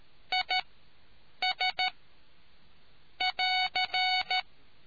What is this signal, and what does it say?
Morse code, heard as a keyed tone, about 13 wpm, spaced as for 7 wpm, 740 Hz. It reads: IS <AR>